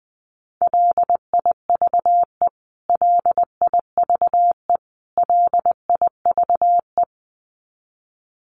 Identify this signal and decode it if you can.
Morse code, heard as a keyed tone, about 20 wpm, 700 Hz. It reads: LIVE LIVE LIVE
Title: Top 200 Words Repeated 20wpm